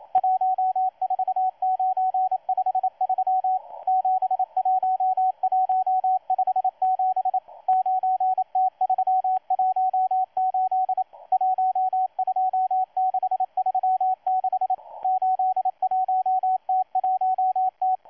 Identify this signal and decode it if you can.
Morse code, heard as a keyed tone, about 28 wpm, 750 Hz, with radio noise. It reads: 14953 71157 9T318 12636 81T1T
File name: M12-2016-07-07-1310utc-13926khz-error-restart